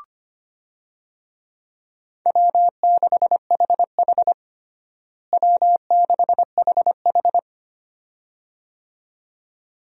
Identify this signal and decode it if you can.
Morse code, heard as a keyed tone, about 25 wpm, 700 Hz. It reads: W6HH W6HH
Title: USA Call Signs - 4 Characters - Repeated 2x - 25wpm